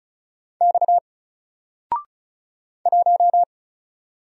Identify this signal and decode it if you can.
Morse code, heard as a keyed tone, about 36 wpm, 700 Hz. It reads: X 1
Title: Single Letter-Number Alt 35wpm